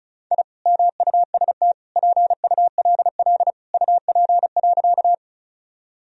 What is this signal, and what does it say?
Morse code, heard as a keyed tone, about 36 wpm, 700 Hz. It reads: I MUST PULL UP.
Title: Sentences from Top 1000 Words - Review Part 2 - 35wpm